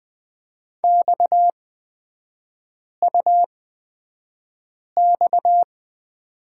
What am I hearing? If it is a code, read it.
Morse code, heard as a keyed tone, about 20 wpm, spaced as for 5 wpm, 690 Hz. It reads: XUX
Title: 229 Lesson 37e - Letter x - Three Characters 20wpm